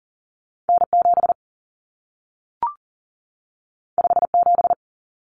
Morse code, heard as a keyed tone, about 40 wpm, 700 Hz. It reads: N7 57
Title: Common Cut Numbers - 2-Digit 40wpm